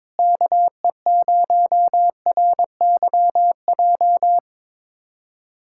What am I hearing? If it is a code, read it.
Morse code, heard as a keyed tone, about 22 wpm, 700 Hz. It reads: KE0RYJ
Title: USA Callsign DE Callsign - 6 Characters - 22wpm